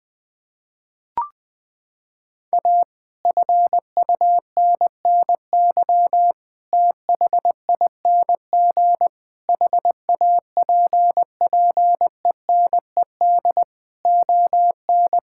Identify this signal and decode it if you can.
Morse code, heard as a keyed tone, about 20 wpm, 700 Hz. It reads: A FUNNY THING HAPPENED ON
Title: Sentences with 8-Word Phrases - 20wpm